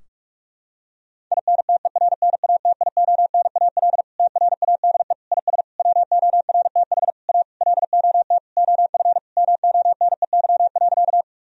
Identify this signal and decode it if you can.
Morse code, heard as a keyed tone, about 41 wpm, 700 Hz. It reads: INTERNATIONAL TRADE IS WORTH A LOT OF MONEY.
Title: Sentences from Top 1000 Words - Review Part 4 - 45wpm